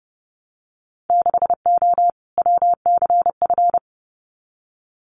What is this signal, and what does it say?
Morse code, heard as a keyed tone, about 30 wpm, 700 Hz. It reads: 6O WCF
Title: Winter Field Day Exchanges Encore 30wpm